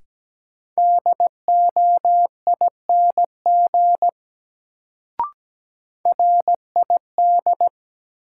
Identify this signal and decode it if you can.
Morse code, heard as a keyed tone, about 17 wpm, 700 Hz. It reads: DOING RID
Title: Sets of 4 Words - New Word Drill - 17wpm